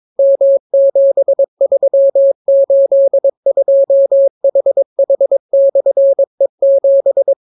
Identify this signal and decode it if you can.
Morse code, heard as a keyed tone, about 22 wpm, 550 Hz. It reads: M7382HH/E7